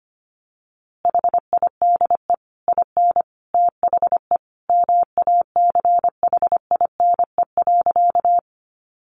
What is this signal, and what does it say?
Morse code, heard as a keyed tone, about 25 wpm, 700 Hz. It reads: HIDE IN THE MACHINE.
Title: A Sound Of Thunder 25wpm